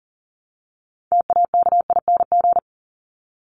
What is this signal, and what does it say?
Morse code, heard as a keyed tone, about 40 wpm, 700 Hz. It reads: TAKING
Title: ING Suffix - Top 100 Words with Base Words 40wpm